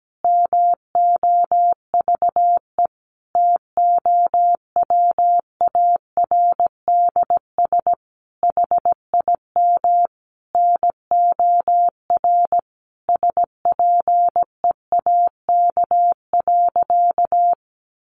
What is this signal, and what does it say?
Morse code, heard as a keyed tone, about 17 wpm, 700 Hz. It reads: MOVE TOWARDS HIM NOR SPEAK.
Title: Sentences from Top 1500 Words 17wpm